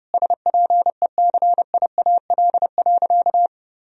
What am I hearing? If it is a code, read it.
Morse code, heard as a keyed tone, about 30 wpm, 700 Hz. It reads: SPECIAL.
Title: Sentences from Top 1000 Words - Review Part 3 - 30wpm